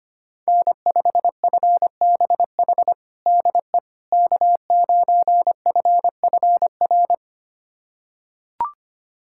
Morse code, heard as a keyed tone, about 25 wpm, 700 Hz. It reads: N5FBH DE K9FFR
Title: USA Callsign DE Callsign - 5 Characters - 25wpm